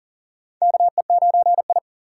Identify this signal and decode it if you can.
Morse code, heard as a keyed tone, about 37 wpm, 700 Hz. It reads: KE9I
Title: K1USN SST QSO Practice - 40wpm